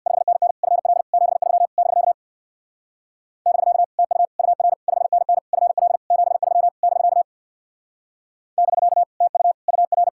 Morse code, heard as a keyed tone, about 64 wpm, 700 Hz. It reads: 5NN FL 73 <BK> <BK> TU UR 5NN FL 73 <BK> <BK> TU UR